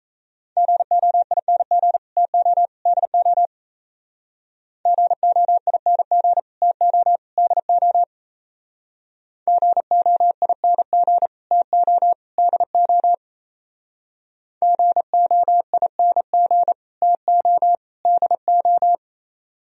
Morse code, heard as a keyed tone, about 36 wpm, 700 Hz. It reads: GOING TO DO GOING TO DO GOING TO DO GOING TO DO